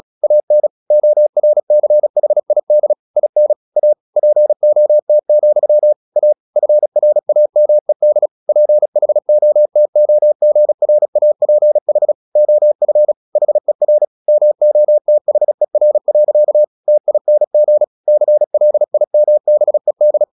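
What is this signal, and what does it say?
Morse code, heard as a keyed tone, about 37 wpm, 600 Hz. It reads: AN ORCHID IN A POT, A FRAMED PHOTOGRAPH OF HER MOTHER. TING CLIMBED
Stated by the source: Morse practice at 36 wpm, text of Ken Follett's novel Never, from CW med Gnister.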